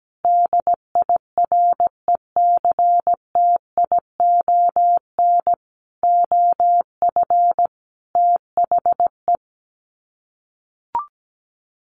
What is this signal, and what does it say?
Morse code, heard as a keyed tone, about 17 wpm, 700 Hz. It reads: DIRECTION OF THE
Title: Sets of 5 Words 17wpm